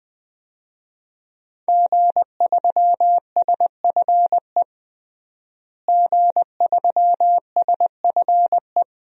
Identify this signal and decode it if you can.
Morse code, heard as a keyed tone, about 20 wpm, 700 Hz. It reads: G3SFE G3SFE
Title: DXCC-100 Call Signs - 5 Characters - Repeated 2x - 20wpm